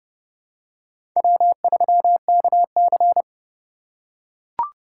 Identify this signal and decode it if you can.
Morse code, heard as a keyed tone, about 30 wpm, 700 Hz. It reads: W3KC